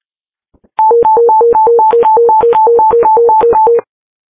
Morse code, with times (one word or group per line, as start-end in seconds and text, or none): none